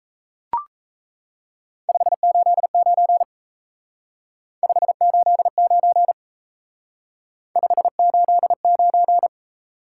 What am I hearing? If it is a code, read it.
Morse code, heard as a keyed tone, about 36 wpm, 700 Hz. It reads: 589 589 589